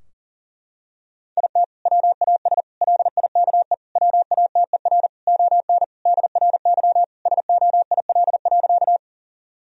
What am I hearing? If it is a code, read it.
Morse code, heard as a keyed tone, about 40 wpm, 700 Hz. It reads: IT WAS LIKE WATER ON DRY SOIL.
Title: Sentences from Top 1000 Words - Review Part 4 - 40wpm